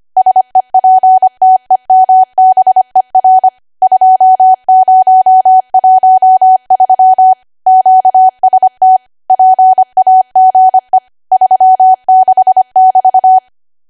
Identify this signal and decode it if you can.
Morse code, heard as a keyed tone, about 25 wpm, 750 Hz. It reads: SEPTEMBER 2013 QST PAGE 36<BT>
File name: cwa / W1AW-Level3 / 25WPM